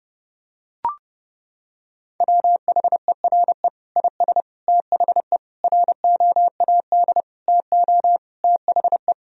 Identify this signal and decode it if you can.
Morse code, heard as a keyed tone, about 30 wpm, 700 Hz. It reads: WHERE IS THE ROAD TO THE
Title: Sentences from Top 1000 Words - Review Part 2 - 30wpm